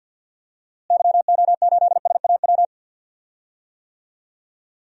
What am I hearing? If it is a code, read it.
Morse code, heard as a keyed tone, about 49 wpm, 700 Hz. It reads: YO8SAW
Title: DXCC-100 Call Signs - 6 Characters - Repeated 3x - 50wpm